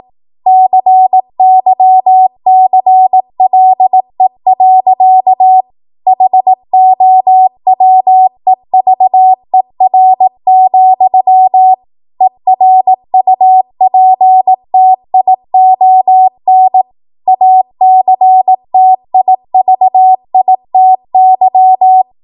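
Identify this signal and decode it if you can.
Morse code, heard as a keyed tone, about 18 wpm, 750 Hz. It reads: CYCLE. HOWEVER, ERUPTION ACTIVITY
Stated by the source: ARRL 18 wpm Morse code practice archive transmission.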